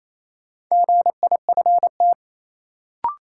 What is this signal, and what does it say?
Morse code, heard as a keyed tone, about 28 wpm, 700 Hz. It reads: GIFT
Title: Sentences from Top 900 Words - New Word Drill - 28wpm